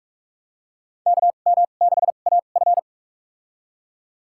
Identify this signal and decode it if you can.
Morse code, heard as a keyed tone, about 41 wpm, spaced as for 31 wpm, 700 Hz. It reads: KM6AF
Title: USA Call Signs - 2x Character Spacing - 45wpm